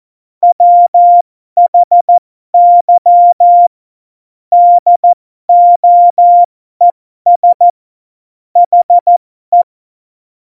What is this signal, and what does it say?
Morse code, heard as a keyed tone, about 14 wpm, spaced as for 12 wpm, 700 Hz. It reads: WHY DOES HE
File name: Morsekurs / Witze